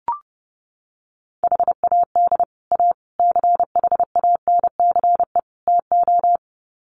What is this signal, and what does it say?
Morse code, heard as a keyed tone, about 30 wpm, 700 Hz. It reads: HAD A CHANCE TO
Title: Sets of 4 Words 30wpm